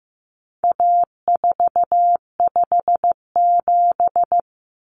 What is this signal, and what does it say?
Morse code, heard as a keyed tone, about 15 wpm, 700 Hz. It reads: A457
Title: Common Cut Numbers - 4-Digit 15wpm